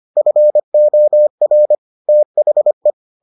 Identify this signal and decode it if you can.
Morse code, heard as a keyed tone, about 25 wpm, 600 Hz. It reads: FOR THE